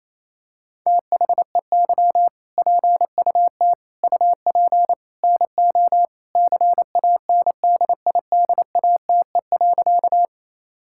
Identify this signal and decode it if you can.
Morse code, heard as a keyed tone, about 28 wpm, 700 Hz. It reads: THEY PUT UP NO CANDIDATE.
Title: Sentences from Top 1600 Words 28wpm